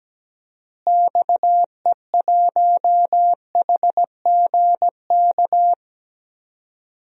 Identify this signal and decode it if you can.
Morse code, heard as a keyed tone, about 17 wpm, 700 Hz. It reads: XE1HGK
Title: DXCC-100 Call Signs - 6 Characters - Repeated 3x - 17wpm